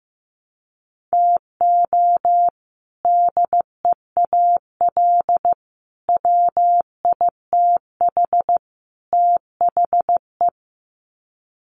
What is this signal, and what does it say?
Morse code, heard as a keyed tone, about 15 wpm, 700 Hz. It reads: TO DEAL WITH THE
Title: Sets of 4 Words 15wpm